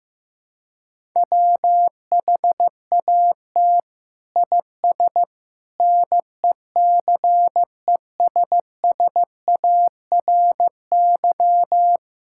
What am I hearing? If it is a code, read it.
Morse code, heard as a keyed tone, about 15 wpm, 700 Hz. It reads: WHAT IS NECESSARY
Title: Sentences from Top 1000 Words - Review Part 5 - 15wpm